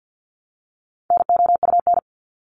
Morse code, heard as a keyed tone, about 50 wpm, 700 Hz. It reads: NOUN
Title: Top 500 Words 50wpm